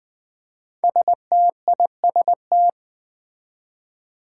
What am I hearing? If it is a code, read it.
Morse code, heard as a keyed tone, about 20 wpm, 700 Hz. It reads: STIST
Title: Five Challenging Letter-Number Combinations 20wpm